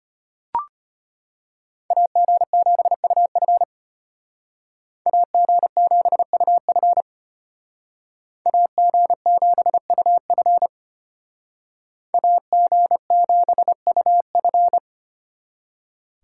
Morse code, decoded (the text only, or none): AG7UF AG7UF AG7UF AG7UF